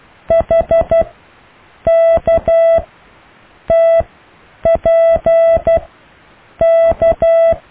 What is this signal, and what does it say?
Morse code, heard as a keyed tone, about 12 wpm, spaced as for 7 wpm, 660 Hz. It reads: HKTPK
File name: cw / EKTPV+BFHNW